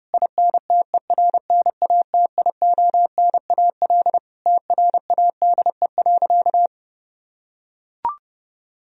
Morse code, heard as a keyed tone, about 30 wpm, 700 Hz. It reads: INTERNATIONAL TRADE.
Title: Sentences from Top 1000 Words - Review Part 4 - 30wpm